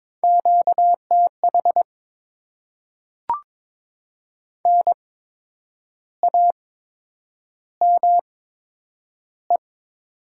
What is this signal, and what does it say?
Morse code, heard as a keyed tone, about 22 wpm, spaced as for 8 wpm, 700 Hz. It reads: QTH NAME
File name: Top 20 QSO Elements - 8x Character Spacing - 22wpm